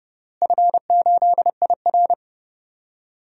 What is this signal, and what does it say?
Morse code, heard as a keyed tone, about 29 wpm, 700 Hz. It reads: F8IR
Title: DXCC-100 Call Signs - 4 Characters - Repeated 3x - 30wpm